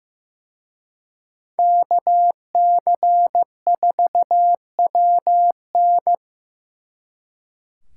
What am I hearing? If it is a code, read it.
Morse code, heard as a keyed tone, about 18 wpm, 700 Hz. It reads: KC4WN
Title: Speed Racing - USA Call Signs 15wpm